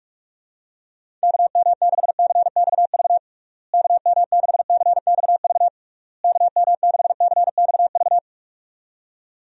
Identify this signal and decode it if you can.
Morse code, heard as a keyed tone, about 44 wpm, 700 Hz. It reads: KM6CXV KM6CXV KM6CXV
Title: USA Call Signs - 6 Characters - Repeated 3x - 45wpm